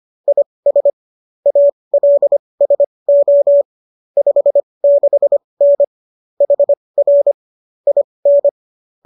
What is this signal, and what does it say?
Morse code, heard as a keyed tone, about 25 wpm, spaced as for 20 wpm, 570 Hz. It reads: IS ALSO 56N HR IN
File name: qso201 20